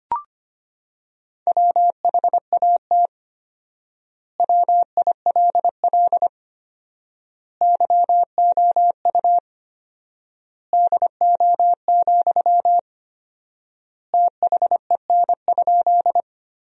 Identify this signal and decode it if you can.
Morse code, heard as a keyed tone, about 25 wpm, 700 Hz. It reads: WHAT WILL YOU DO, THEN?
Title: 4x Word-Spacing - Sentences from Top 100 Words - 25wpm